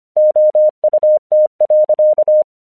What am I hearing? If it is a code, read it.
Morse code, heard as a keyed tone, about 25 wpm, 610 Hz. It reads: OUT.
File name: 1 English 25wpm / AQC05